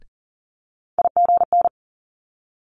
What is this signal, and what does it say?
Morse code, heard as a keyed tone, about 40 wpm, 700 Hz. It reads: IGN
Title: Multi-letter Phonemes Quick Fire 40wpm